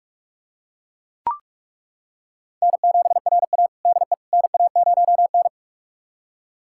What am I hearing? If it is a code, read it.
Morse code, heard as a keyed tone, about 41 wpm, 700 Hz. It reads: N7RA DE NA0N
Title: USA Callsign DE Callsign - 4 Characters - 45wpm